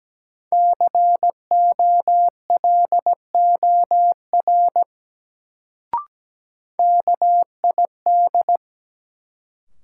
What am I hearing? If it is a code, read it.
Morse code, heard as a keyed tone, about 17 wpm, 700 Hz. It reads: COLOR KID
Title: Sentences from Top 1600 Words - New Word Drill Alt - 17wpm